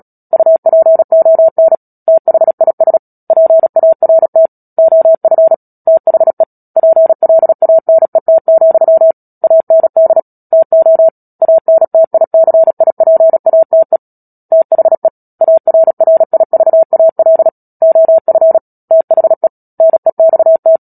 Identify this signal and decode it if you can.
Morse code, heard as a keyed tone, about 37 wpm, 650 Hz. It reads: UPON THIS PART OF THE PLANET, AND TO ANTICIPATE THE ARRIVAL OF THE NEXT
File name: Book Two / 10-Chapter 10 - Epilogue - 36 WPM